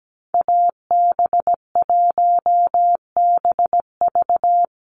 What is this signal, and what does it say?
Morse code, heard as a keyed tone, about 17 wpm, 700 Hz. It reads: AB1BV
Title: USA Call Signs 17wpm